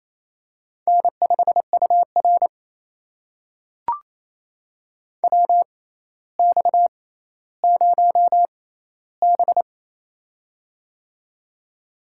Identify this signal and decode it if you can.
Morse code, heard as a keyed tone, about 28 wpm, spaced as for 14 wpm, 700 Hz. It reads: N5UR WX0B
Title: USA Call Signs - 4 Characters - 6x Character Spacing - 28wpm